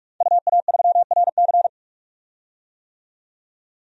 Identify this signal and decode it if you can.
Morse code, heard as a keyed tone, about 42 wpm, 700 Hz. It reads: UA3RC
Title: DXCC-100 Call Signs - 5 Characters - 45wpm